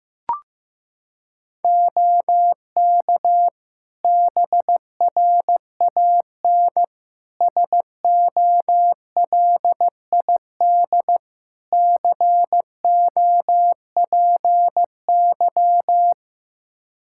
Morse code, heard as a keyed tone, about 15 wpm, 700 Hz. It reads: OK BRAN SOLID COPY
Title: Practice QSO - Exchange 3 - More Info - 15wpm